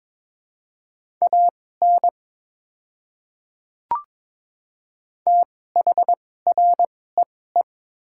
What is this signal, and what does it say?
Morse code, heard as a keyed tone, about 22 wpm, spaced as for 16 wpm, 700 Hz. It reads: AN THREE
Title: 2x Character Spacing - Top 500 Words - 22wpm